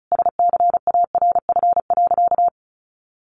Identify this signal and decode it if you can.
Morse code, heard as a keyed tone, about 34 wpm, 700 Hz. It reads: SCARF.
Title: Easy Phrases 35wpm